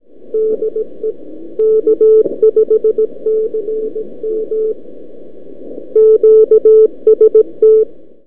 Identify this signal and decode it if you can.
Morse code, heard as a keyed tone, about 17 wpm, 440 Hz. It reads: DE K5CM QST
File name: k5cm 40m 2-2-12 by abi